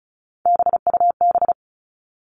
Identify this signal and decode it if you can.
Morse code, heard as a keyed tone, about 34 wpm, 700 Hz. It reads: BUB
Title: All Cut Numbers - 3-Digit 35wpm